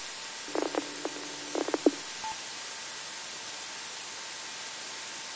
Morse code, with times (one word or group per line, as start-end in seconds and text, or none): none